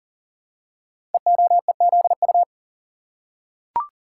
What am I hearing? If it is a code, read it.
Morse code, heard as a keyed tone, about 39 wpm, 690 Hz. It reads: EOEZU
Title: Code Groups - 5 Letters - 40wpm